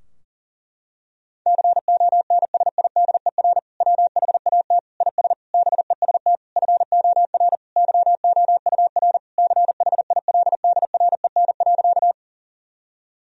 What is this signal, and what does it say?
Morse code, heard as a keyed tone, about 37 wpm, 700 Hz. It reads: CONSIDER WHAT IS BEST FOR YOUR CHILDREN.
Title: Sentences from Top 1000 Words - Review Part 4 - 40wpm